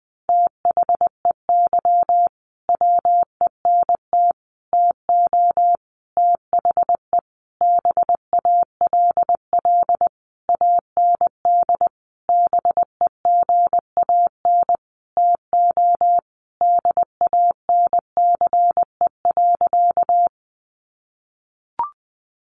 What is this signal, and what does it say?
Morse code, heard as a keyed tone, about 19 wpm, 700 Hz. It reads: THEY WENT TO THE BALL AND BEGAN TO DANCE.
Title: Sentences from Top 900 Words 20wpm